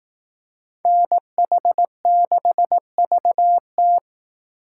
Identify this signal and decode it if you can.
Morse code, heard as a keyed tone, about 19 wpm, 700 Hz. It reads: NH6VT
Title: Speed Racing - USA 2x2 Call Signs 15wpm